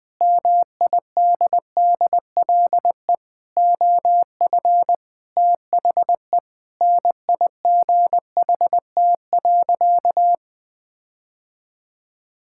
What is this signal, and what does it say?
Morse code, heard as a keyed tone, about 20 wpm, 700 Hz. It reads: MIDDLE OF THE NIGHT.
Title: Sentences with 8-Word Phrases - Spoken Beginning - 20wpm